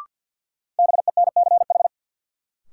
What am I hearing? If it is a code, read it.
Morse code, heard as a keyed tone, about 52 wpm, 700 Hz. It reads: BENCH
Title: Sentences from Top 1400 Words - New Word Drill Alt - 50wpm